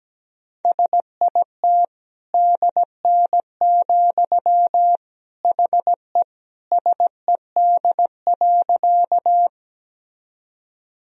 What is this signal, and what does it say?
Morse code, heard as a keyed tone, about 17 wpm, 700 Hz. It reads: SIT DN, HE SED.
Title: Sentences from Top 500 Words and Abbreviations 17wpm